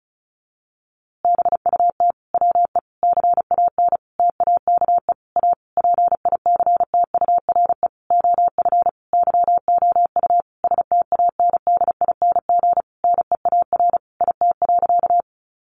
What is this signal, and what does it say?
Morse code, heard as a keyed tone, about 34 wpm, 700 Hz. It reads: BUT WE CAN TAKE A PICTURE OF YOU STANDING NEAR IT.
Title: A Sound Of Thunder 35wpm